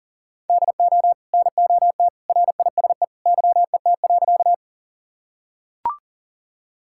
Morse code, heard as a keyed tone, about 40 wpm, 700 Hz. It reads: DO NOT RISE YET.
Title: Sentences from Top 1000 Words - Review Part 6 - 40wpm